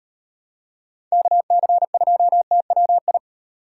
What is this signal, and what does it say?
Morse code, heard as a keyed tone, about 36 wpm, 700 Hz. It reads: KC2TWI